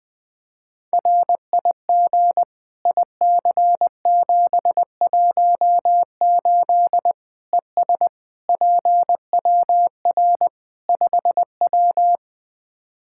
Morse code, heard as a keyed tone, about 20 wpm, 700 Hz. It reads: RIG IC718 ES PWR 5W
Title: Practice QSO - Exchange 2 - Info - 20wpm